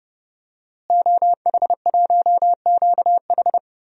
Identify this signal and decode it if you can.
Morse code, heard as a keyed tone, about 30 wpm, 700 Hz. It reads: OH1QH
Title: DXCC-100 Call Signs - 5 Characters - Repeated 3x - 30wpm